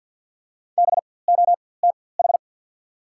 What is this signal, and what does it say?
Morse code, heard as a keyed tone, about 54 wpm, spaced as for 24 wpm, 690 Hz. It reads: BOTH
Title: 4x Character Spacing - Top 300 Words - 50wpm